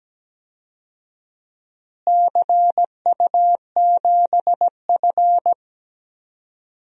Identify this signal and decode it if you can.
Morse code, heard as a keyed tone, about 17 wpm, 700 Hz. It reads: CU7F
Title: DXCC-100 Call Signs - 4 Characters - Repeated 2x - 17wpm